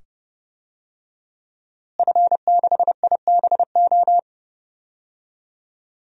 Morse code, heard as a keyed tone, about 30 wpm, 700 Hz. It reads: F6IBO
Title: DXCC-100 Call Signs - 5 Characters - Repeated 2x - 30wpm